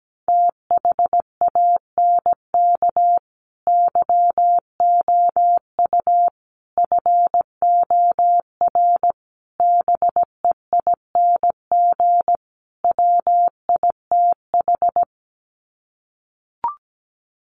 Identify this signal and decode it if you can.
Morse code, heard as a keyed tone, about 17 wpm, 700 Hz. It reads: THANK YOU FOR BEING WITH
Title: Sets of 5 Words 17wpm